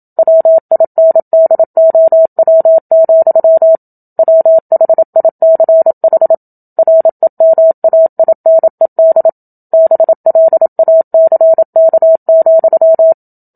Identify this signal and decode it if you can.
Morse code, heard as a keyed tone, about 27 wpm, 650 Hz. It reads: WINDOW, WHICH REMAINED BLACK,
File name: Book Two / 01-Chapter 1 - 27 WPM